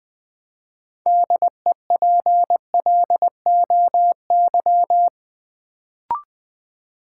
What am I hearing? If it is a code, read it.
Morse code, heard as a keyed tone, about 20 wpm, 700 Hz. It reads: DEPLOY
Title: Build A Word - 3 Letters At A Time - 6-Letter Words - 20wpm